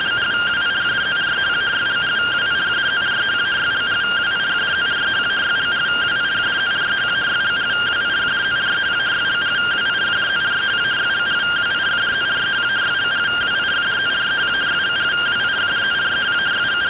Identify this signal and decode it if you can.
Two-tone teleprinter signal, about 170 Hz shift.
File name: findings / SAGITAL nb203 fr61 3mm.gen-30 rtty rx